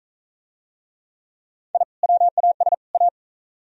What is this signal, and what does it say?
Morse code, heard as a keyed tone, about 39 wpm, 700 Hz. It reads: I WAS A